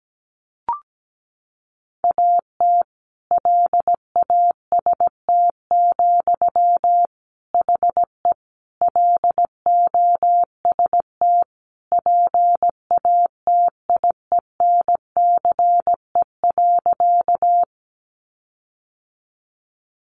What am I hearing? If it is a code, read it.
Morse code, heard as a keyed tone, about 17 wpm, 700 Hz. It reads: AT LAST, HE LOST PATIENCE.
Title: Sentences from Top 1400 Words 17wpm